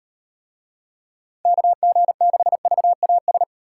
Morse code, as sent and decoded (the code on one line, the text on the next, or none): -.- --. -.... ...- .- ...
KG6VAS